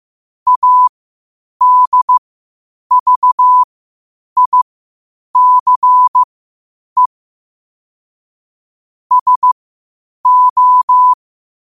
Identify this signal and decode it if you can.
Morse code, heard as a keyed tone, about 15 wpm, spaced as for 8 wpm, 1000 Hz. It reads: ADVICE SO
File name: qotdfiles / 8wpm